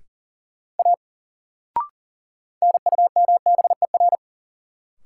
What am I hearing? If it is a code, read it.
Morse code, heard as a keyed tone, about 40 wpm, 700 Hz. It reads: A NUMBER
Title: Sentences from Top 100 Words - New Word Drill - 40wpm